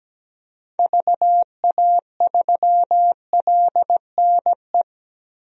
Morse code, heard as a keyed tone, about 17 wpm, 700 Hz. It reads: VA3LNE
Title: DXCC-100 Call Signs - 6 Characters - 17wpm